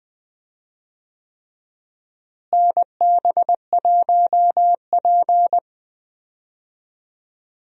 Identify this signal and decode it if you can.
Morse code, heard as a keyed tone, about 21 wpm, 700 Hz. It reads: NB1P
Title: USA Call Signs - 4 Characters - Repeated 3x - 20wpm